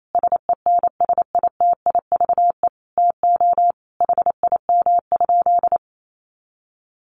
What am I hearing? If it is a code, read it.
Morse code, heard as a keyed tone, about 28 wpm, 700 Hz. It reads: SENSITIVE TO HIM?
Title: Sentences from Top 1400 Words 28wpm